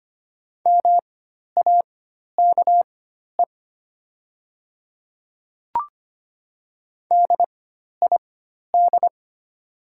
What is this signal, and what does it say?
Morse code, heard as a keyed tone, about 25 wpm, spaced as for 12 wpm, 700 Hz. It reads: MAKE DID
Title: 4x Character Spacing - Top 200 Words - 25wpm